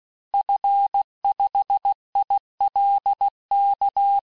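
Morse code, heard as a keyed tone, about 16 wpm, 800 Hz. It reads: F5ILK
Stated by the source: Morse course text, end of QSO, sent at 15 words per minute.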